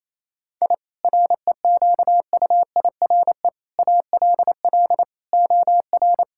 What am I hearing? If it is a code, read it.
Morse code, heard as a keyed tone, about 27 wpm, 700 Hz. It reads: I REQUIRE ALL OR
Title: Sentences from Top 1000 Words - Review Part 6 - 28wpm